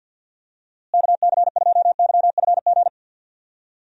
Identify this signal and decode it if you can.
Morse code, heard as a keyed tone, about 51 wpm, 700 Hz. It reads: KC2YFG
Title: USA Call Signs - 6 Characters - Repeated 2x - 50wpm